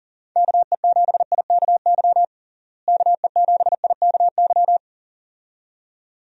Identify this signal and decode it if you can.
Morse code, heard as a keyed tone, about 40 wpm, 700 Hz. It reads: KE7IKY KE7IKY
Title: USA Call Signs - 6 Characters - Repeated 2x - 40wpm